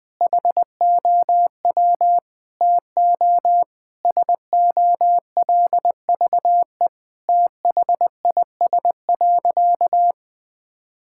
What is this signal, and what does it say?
Morse code, heard as a keyed tone, about 20 wpm, 700 Hz. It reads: HOW TO SOLVE THIS.
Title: Sentences with 5-Word Phrases - 20wpm